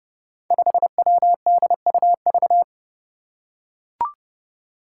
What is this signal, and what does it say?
Morse code, heard as a keyed tone, about 30 wpm, 700 Hz. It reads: 5WDUV
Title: Five Challenging Letter-Number Combinations 30wpm